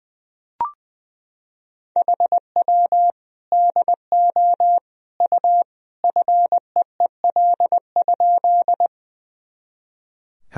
Morse code, heard as a keyed tone, about 20 wpm, 700 Hz. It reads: HW DO U FEEL?
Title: Sentences from Top 400 Words and Abbreviations 20wpm